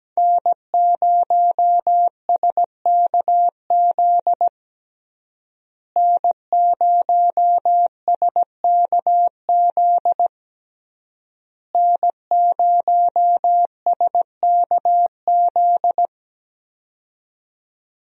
Morse code, heard as a keyed tone, about 17 wpm, 700 Hz. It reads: N0SKZ N0SKZ N0SKZ
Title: USA Call Signs - 5 Characters - Repeated 3x - 17wpm